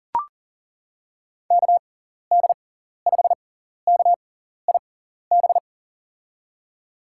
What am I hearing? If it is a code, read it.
Morse code, heard as a keyed tone, about 40 wpm, spaced as for 14 wpm, 700 Hz. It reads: KD5KIB
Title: USA Call Signs - 6 Characters - 6x Character Spacing - 40wpm